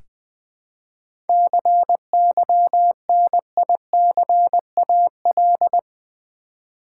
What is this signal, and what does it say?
Morse code, heard as a keyed tone, about 20 wpm, 700 Hz. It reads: CYNICAL
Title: 7 Letter Words - First 3 Letters Given - 20wpm